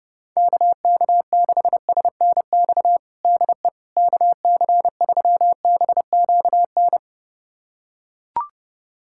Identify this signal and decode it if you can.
Morse code, heard as a keyed tone, about 30 wpm, 700 Hz. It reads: KK6SNX DE KC3BQN
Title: USA Callsign DE Callsign - 6 Characters - 30wpm